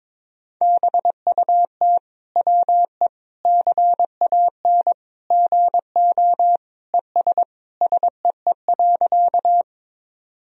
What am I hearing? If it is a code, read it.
Morse code, heard as a keyed tone, about 21 wpm, 700 Hz. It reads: BUT WE CAN GO ES SEE.
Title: Sentences from Top 100 Words and Abbreviations 22wpm